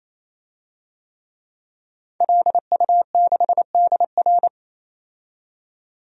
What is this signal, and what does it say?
Morse code, heard as a keyed tone, about 28 wpm, 700 Hz. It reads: LU6DR
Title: DXCC-100 Call Signs - 5 Characters - 28wpm